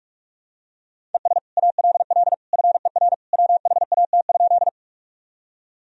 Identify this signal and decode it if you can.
Morse code, heard as a keyed tone, about 42 wpm, 700 Hz. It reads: ES ALL FER WHAT?
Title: Sentences from Top 100 Words and Abbreviations 45wpm